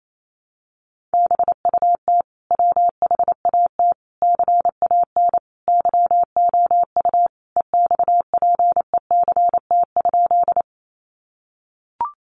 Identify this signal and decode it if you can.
Morse code, heard as a keyed tone, about 28 wpm, 700 Hz. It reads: BUT WHAT CAN YOU EXPECT?
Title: Sentences from Top 1100 Words 28wpm